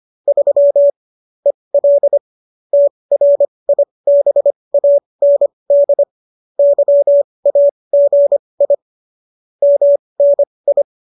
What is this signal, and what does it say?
Morse code, heard as a keyed tone, about 25 wpm, spaced as for 20 wpm, 570 Hz. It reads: 3 EL TRIBAND YAGI MNI